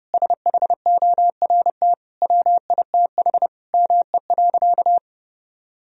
Morse code, heard as a keyed tone, about 32 wpm, 700 Hz. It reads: SHORT WITH ME.
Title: Sentences from Top 1000 Words - Review Part 2 - 30wpm